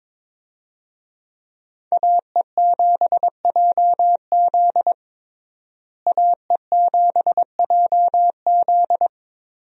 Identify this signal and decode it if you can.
Morse code, heard as a keyed tone, about 22 wpm, 700 Hz. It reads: AE7JZ AE7JZ
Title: USA Call Signs - 5 Characters - Repeated 3x - 22wpm